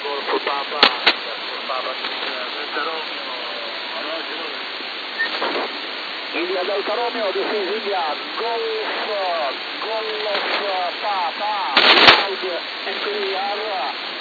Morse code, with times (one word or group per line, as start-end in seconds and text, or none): none